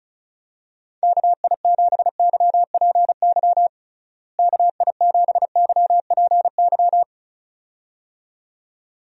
Morse code, heard as a keyed tone, about 34 wpm, 700 Hz. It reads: KI7YPY KI7YPY
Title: USA Call Signs - 6 Characters - Repeated 2x - 35wpm